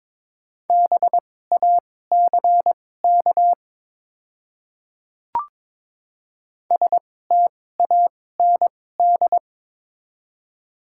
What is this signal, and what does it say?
Morse code, heard as a keyed tone, about 22 wpm, spaced as for 16 wpm, 700 Hz. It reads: BACK STAND
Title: 2x Character Spacing - Top 300 Words - 22wpm